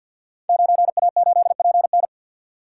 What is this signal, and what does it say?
Morse code, heard as a keyed tone, about 52 wpm, 700 Hz. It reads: 9A8PN
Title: Code Groups - 5 Letter:Numbers - 50wpm